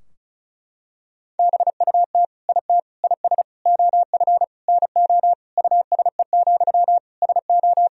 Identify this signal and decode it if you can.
Morse code, heard as a keyed tone, about 34 wpm, 700 Hz. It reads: BUT IT IS OF NO USE, SO
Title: Sentences from Top 1000 Words - Review Part 2 - 35wpm